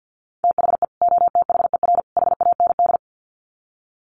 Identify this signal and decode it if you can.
Morse code, heard as a keyed tone, about 50 wpm, 700 Hz. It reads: THE OTHER HAND
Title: Sets of 3 Words 50wpm